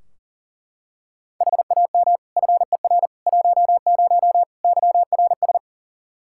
Morse code, heard as a keyed tone, about 40 wpm, 700 Hz. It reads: HAM FER 10 YRS
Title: Practice QSO - Exchange 3 - More Info - 40wpm